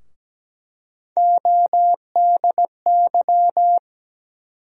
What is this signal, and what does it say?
Morse code, heard as a keyed tone, about 17 wpm, 700 Hz. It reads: ODY
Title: USA Call Sign Postfixes - 3 Letters 17wpm